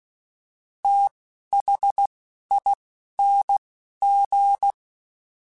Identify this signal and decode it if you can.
Morse code, heard as a keyed tone, about 16 wpm, spaced as for 11 wpm, 800 Hz. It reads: THING
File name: TH / TH mp3s / TH Prefixes 3x2 Twice @10